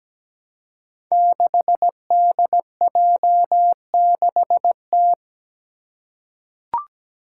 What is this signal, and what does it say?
Morse code, heard as a keyed tone, about 17 wpm, 700 Hz. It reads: 6DJ6T
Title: Five Challenging Letter-Number Combinations 17wpm